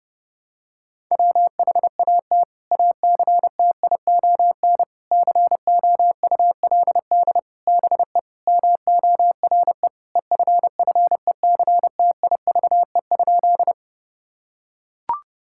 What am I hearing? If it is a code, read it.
Morse code, heard as a keyed tone, about 30 wpm, 700 Hz. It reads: WHAT ACTION COULD BE MORE EFFECTIVE?
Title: Sentences from Top 1000 Words - Review Part 6 - 30wpm